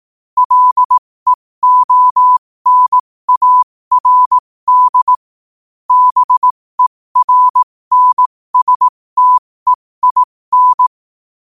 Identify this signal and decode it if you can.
Morse code, heard as a keyed tone, about 18 wpm, spaced as for 15 wpm, 1000 Hz. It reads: LEONARD BERNSTEIN
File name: qotdfiles / 15wpm